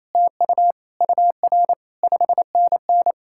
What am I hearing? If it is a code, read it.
Morse code, heard as a keyed tone, about 38 wpm, 700 Hz. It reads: TU UR 5NN